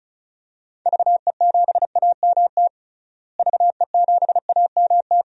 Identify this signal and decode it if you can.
Morse code, heard as a keyed tone, about 36 wpm, 700 Hz. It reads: VE7AMT VE7AMT
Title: DXCC-100 Call Signs - 6 Characters - Repeated 2x - 35wpm